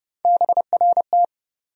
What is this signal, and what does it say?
Morse code, heard as a keyed tone, about 30 wpm, 700 Hz. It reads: BRT
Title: USA Call Sign Postfixes - 3 Letters 30wpm